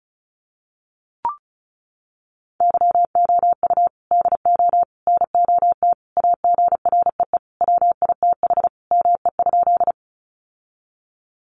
Morse code, heard as a keyed tone, about 34 wpm, 700 Hz. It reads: YOU DO NOT AGREE WITH ME?
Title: Sentences from Top 1100 Words 35wpm